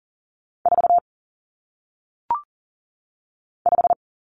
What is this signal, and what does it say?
Morse code, heard as a keyed tone, about 39 wpm, 700 Hz. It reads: 4 5
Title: Numbers - 1-Digit 40wpm